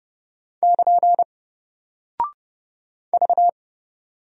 Morse code, heard as a keyed tone, about 30 wpm, 700 Hz. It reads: <KN> V
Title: Single Character (Letter:Number:Punctuation:Prosign) Alt 30wpm